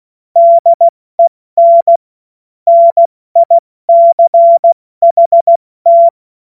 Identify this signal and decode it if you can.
Morse code, heard as a keyed tone, about 16 wpm, spaced as for 14 wpm, 680 Hz. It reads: DEN NICHT
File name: Witz-12-August-2025-70-F80